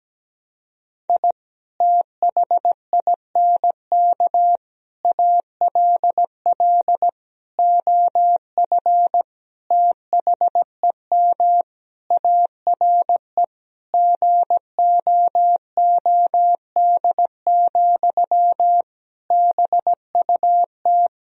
Morse code, heard as a keyed tone, about 17 wpm, 700 Hz. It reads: I THINK ALL OF THEM ARE GOOD, BUT
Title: Sentences from Top 1000 Words - Review Part 4 - 17wpm